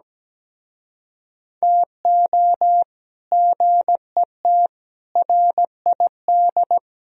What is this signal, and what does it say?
Morse code, heard as a keyed tone, about 17 wpm, 700 Hz. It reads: TO GET RID
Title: Speed-Racing - Sets of 5 Words 17wpm